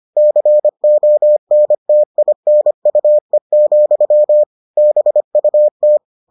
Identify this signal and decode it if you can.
Morse code, heard as a keyed tone, about 25 wpm, 600 Hz. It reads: CONTINUE, BUT